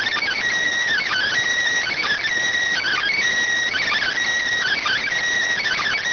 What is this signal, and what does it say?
Emergency Alert System (SAME) tones, heard as harsh buzzing bursts. It is heard from 0.9 to 5.0 s.